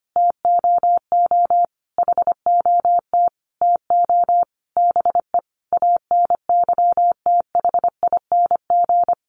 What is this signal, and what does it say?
Morse code, heard as a keyed tone, about 25 wpm, 700 Hz. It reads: TOO HOT TO BE ANYTHING
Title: Sentences from Top 1200 Words 25wpm